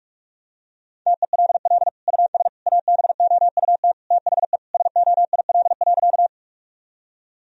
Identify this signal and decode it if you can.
Morse code, heard as a keyed tone, about 46 wpm, 700 Hz. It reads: TELL US ABOUT THE SOIL.
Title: Sentences from Top 1000 Words - Review Part 4 - 45wpm